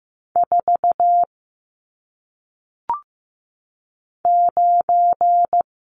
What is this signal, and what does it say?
Morse code, heard as a keyed tone, about 15 wpm, 700 Hz. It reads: 4 9
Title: Numbers - 1-Digit 15wpm